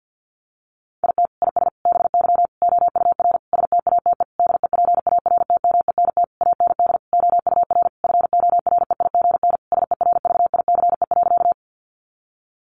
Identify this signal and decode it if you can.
Morse code, heard as a keyed tone, about 50 wpm, 700 Hz. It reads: IT IS BY OUR STATE DEPARTMENT AND OUR FOREIGN SERVICE.
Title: Sentences from Top 1400 Words 50wpm